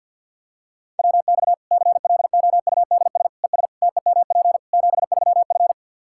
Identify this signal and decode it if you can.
Morse code, heard as a keyed tone, about 47 wpm, 700 Hz. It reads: WX CLOUDS ES TEMP 73F
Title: Practice QSO - Exchange 2 - Info - 50wpm